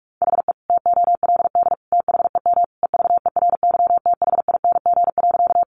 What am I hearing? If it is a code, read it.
Morse code, heard as a keyed tone, about 44 wpm, 700 Hz. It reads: HE TOLD THEM EVERYTHING.
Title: Sentences from Top 700 Words 45wpm